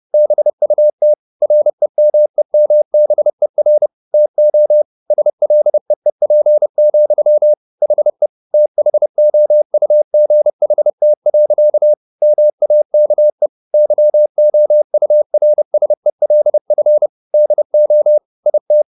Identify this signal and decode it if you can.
Morse code, heard as a keyed tone, about 30 wpm, 600 Hz. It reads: BUT REMEMBER TO SLEEP, HE THOUGHT. MAKE YOURSELF DO IT